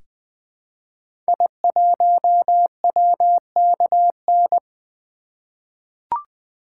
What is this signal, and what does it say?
Morse code, heard as a keyed tone, about 21 wpm, 700 Hz. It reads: I1WKN
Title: 2023 SOTA Activator Call Signs - 20wpm